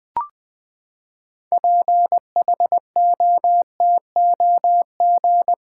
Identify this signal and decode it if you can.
Morse code, heard as a keyed tone, about 20 wpm, 700 Hz. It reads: PHOTOG
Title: Build A Word - 6 Letters At A Time - 12-Letter Words - 20wpm